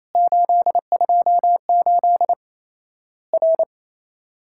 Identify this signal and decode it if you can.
Morse code, heard as a keyed tone, about 27 wpm, 700 Hz. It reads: 828 R
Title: Worked All Europe DX Contest - QTC Exchange - 28wpm